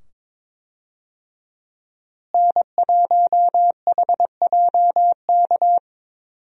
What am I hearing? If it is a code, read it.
Morse code, heard as a keyed tone, about 22 wpm, 700 Hz. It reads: N1HJK
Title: USA Call Signs - 5 Characters - Repeated 2x - 22wpm